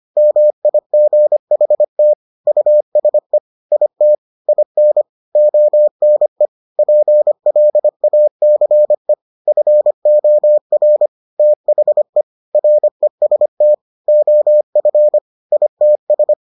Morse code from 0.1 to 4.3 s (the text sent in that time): MIGHT USE IT